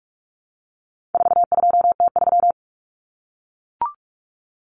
Morse code, heard as a keyed tone, about 44 wpm, 700 Hz. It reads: 42T3
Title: Common Cut Numbers - 4-Digit 45wpm